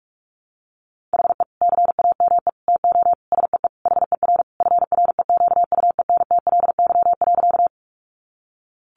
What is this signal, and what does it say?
Morse code, heard as a keyed tone, about 44 wpm, 700 Hz. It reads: HE CAME TO SEE HER FREQUENTLY.
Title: Sentences from Top 1300 Words 45wpm